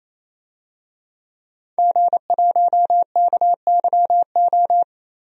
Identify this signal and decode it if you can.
Morse code, heard as a keyed tone, about 28 wpm, 700 Hz. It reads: G1KYO
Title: DXCC-100 Call Signs - 5 Characters - 28wpm